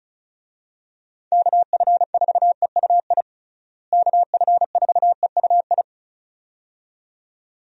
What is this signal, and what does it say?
Morse code, heard as a keyed tone, about 36 wpm, 700 Hz. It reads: KF4EUI KF4EUI
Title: USA Call Signs - 6 Characters - Repeated 2x - 35wpm